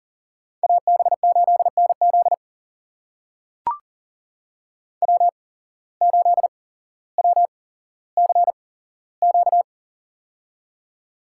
Morse code, heard as a keyed tone, about 40 wpm, spaced as for 17 wpm, 700 Hz. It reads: AB8NZ W8WCQ
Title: USA Call Signs - 5 Characters - 8x Character Spacing - 40wpm